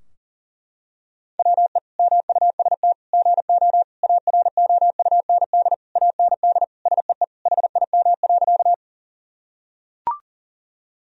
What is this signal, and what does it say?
Morse code, heard as a keyed tone, about 40 wpm, 700 Hz. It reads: WE MUST GO AROUND AND SEE HIM.
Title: Sentences from Top 1000 Words - Review Part 3 - 40wpm